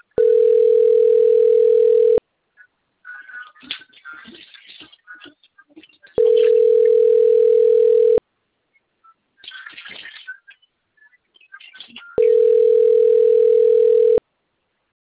Ringback tone (North America), 440 + 480 Hz.
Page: 177.136.239.36 Limeira IV